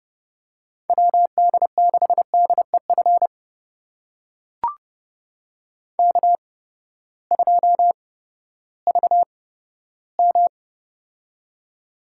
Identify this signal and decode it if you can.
Morse code, heard as a keyed tone, about 28 wpm, spaced as for 15 wpm, 700 Hz. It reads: WD6DEF K2VM
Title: USA Call Signs - 8x Character Spacing - 30wpm